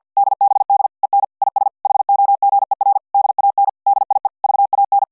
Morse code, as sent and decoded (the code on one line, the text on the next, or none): -.. -... -.. / . -. / .. ... / .... --- --. .-. / -.. .- -. / -.. .. / ...- .- -.
DBD EN IS HOGER DAN DIE VAN